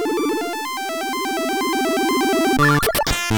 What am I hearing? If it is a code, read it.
There is no code.